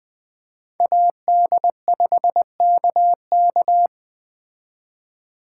Morse code, heard as a keyed tone, about 20 wpm, 700 Hz. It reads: AD5KK
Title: Speed Racing - USA 2x2 Call Signs 17wpm